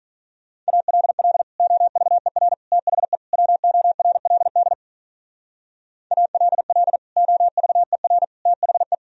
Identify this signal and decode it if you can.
Morse code, heard as a keyed tone, about 43 wpm, 700 Hz. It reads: ALL OVER THE WORLD ALL OVER THE
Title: Speed-Racing - Sets of 4 Words 35wpm